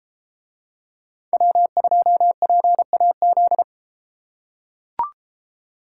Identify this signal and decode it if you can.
Morse code, heard as a keyed tone, about 33 wpm, 700 Hz. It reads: W2PAZ